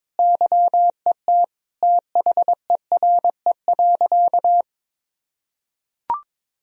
Morse code, heard as a keyed tone, about 22 wpm, 700 Hz. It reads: YET THERE.
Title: Sentences from Top 1000 Words - Review Part 1 - 22wpm